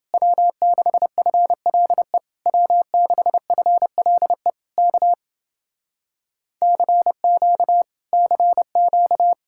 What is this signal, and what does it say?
Morse code, heard as a keyed tone, about 30 wpm, 700 Hz. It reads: W6FLE W6FLE K CQ CQ